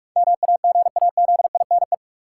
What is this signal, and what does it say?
Morse code, heard as a keyed tone, about 44 wpm, 700 Hz. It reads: MAGAZINE
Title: Sentences from Top 1000 Words - Review Part 7 - New Word Drill - 45wpm